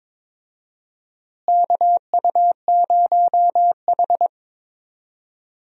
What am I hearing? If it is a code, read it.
Morse code, heard as a keyed tone, about 22 wpm, 700 Hz. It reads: KU0H
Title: USA Call Signs - 4 Characters - Repeated 3x - 22wpm